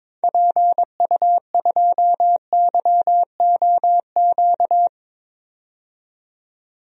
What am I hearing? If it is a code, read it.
Morse code, heard as a keyed tone, about 22 wpm, 700 Hz. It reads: PU2YOQ
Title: DXCC-100 Call Signs - 6 Characters - 22wpm